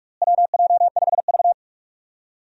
Morse code, heard as a keyed tone, about 46 wpm, 700 Hz. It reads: WJ5V
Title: USA Call Signs - 4 Characters - Repeated 2x - 45wpm